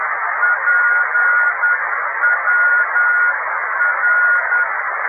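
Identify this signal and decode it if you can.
Morse code, heard as a keyed tone, about 10 wpm, 1440 Hz, with radio noise. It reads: PWR
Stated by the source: Italian amateur beacon IW0HK on 28322 kHz.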